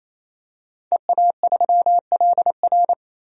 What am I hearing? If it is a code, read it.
Morse code, heard as a keyed tone, about 28 wpm, 700 Hz. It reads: EA3LR
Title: DXCC-100 Call Signs - 5 Characters - 28wpm